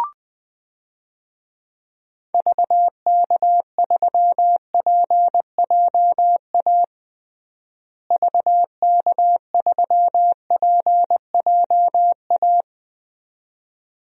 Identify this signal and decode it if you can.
Morse code, heard as a keyed tone, about 21 wpm, 700 Hz. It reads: VK3PJA VK3PJA
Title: DXCC-100 Call Signs - 6 Characters - Repeated 2x - 20wpm